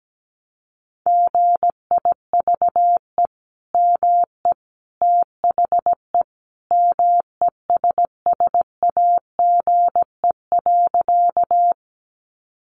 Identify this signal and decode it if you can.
Morse code, heard as a keyed tone, about 17 wpm, 700 Hz. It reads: GIVE ME THE MESSAGE.
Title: Sentences from Top 1100 Words 17wpm